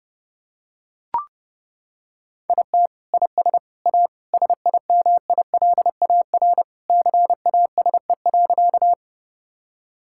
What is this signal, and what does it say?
Morse code, heard as a keyed tone, about 30 wpm, 700 Hz. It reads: IT IS A SIMILAR CASE.
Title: Sentences from Top 1000 Words - Review Part 5 - 30wpm